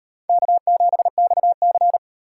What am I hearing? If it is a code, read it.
Morse code, heard as a keyed tone, about 36 wpm, 700 Hz. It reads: K7XC